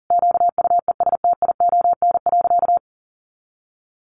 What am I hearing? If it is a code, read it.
Morse code, heard as a keyed tone, about 40 wpm, 700 Hz. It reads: QUESTION.
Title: Sentences from Top 1400 Words 40wpm